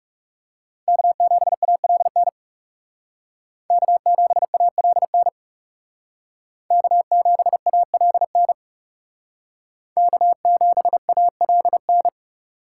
Morse code, heard as a keyed tone, about 38 wpm, 700 Hz. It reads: K7ALN K7ALN K7ALN K7ALN